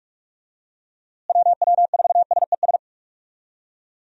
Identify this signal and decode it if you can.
Morse code, heard as a keyed tone, about 46 wpm, 700 Hz. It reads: WW4SES
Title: USA Call Signs - 6 Characters - Repeated 2x - 45wpm